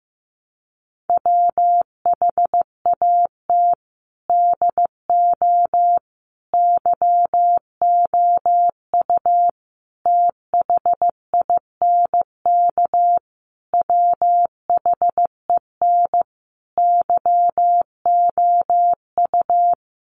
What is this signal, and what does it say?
Morse code, heard as a keyed tone, about 15 wpm, 700 Hz. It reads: WHAT DO YOU THINK WHEN YOU
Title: Sentences from Top 300 Words 15wpm